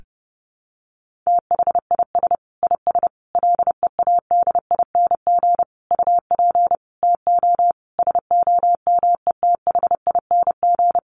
Morse code, heard as a keyed tone, about 30 wpm, 700 Hz. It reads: THIS IS LEADING UP TO SOMETHING
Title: Sentences from Top 1100 Words 30wpm